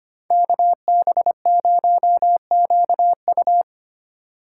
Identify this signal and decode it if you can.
Morse code, heard as a keyed tone, about 33 wpm, 700 Hz. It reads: KB0QU